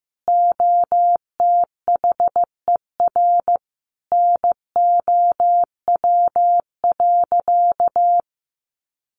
Morse code, heard as a keyed tone, about 15 wpm, 700 Hz. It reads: OTHER NOW.
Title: Sentences from Top 1200 Words 15wpm